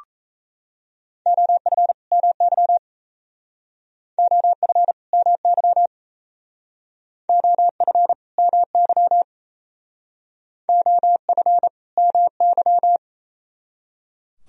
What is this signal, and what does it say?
Morse code, heard as a keyed tone, about 35 wpm, 700 Hz. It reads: OF MY OF MY OF MY OF MY
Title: Speed-Racing - Sets of 2 Words 28wpm